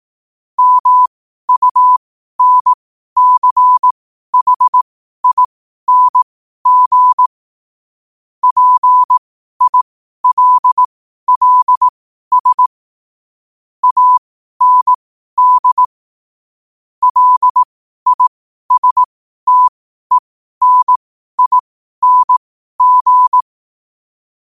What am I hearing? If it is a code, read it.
Morse code, heard as a keyed tone, about 18 wpm, spaced as for 12 wpm, 1000 Hz. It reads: MUNCHING PILLS AND LISTENING